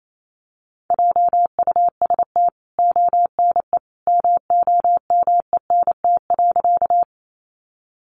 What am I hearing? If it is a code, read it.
Morse code, heard as a keyed tone, about 28 wpm, 700 Hz. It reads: JUST ONE MOMENT.
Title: Sentences from Top 1000 Words 28wpm